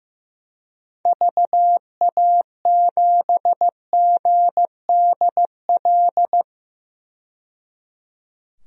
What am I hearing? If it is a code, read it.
Morse code, heard as a keyed tone, about 15 wpm, 700 Hz. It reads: VA7GDL
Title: DXCC-100 Call Signs - 6 Characters - 15wpm